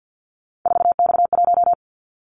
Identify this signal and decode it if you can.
Morse code, heard as a keyed tone, about 50 wpm, 700 Hz. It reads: 4X1
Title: Callsign Prefixes for Top 50 DXCC - 50wpm